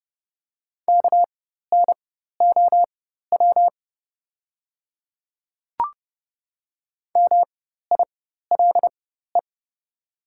Morse code, heard as a keyed tone, about 30 wpm, spaced as for 14 wpm, 700 Hz. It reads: KNOW MILE
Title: 4x Character Spacing - Top 500 Words - 30wpm